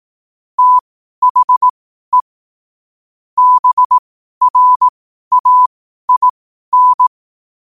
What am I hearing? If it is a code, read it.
Morse code, heard as a keyed tone, about 18 wpm, spaced as for 12 wpm, 1000 Hz. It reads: THE BRAIN
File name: qotdfiles / 12wpm